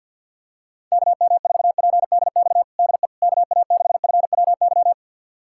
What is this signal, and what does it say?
Morse code, heard as a keyed tone, about 50 wpm, 700 Hz. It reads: KM4PDX DE KA6FWY
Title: USA Callsign DE Callsign - 6 Characters - 50wpm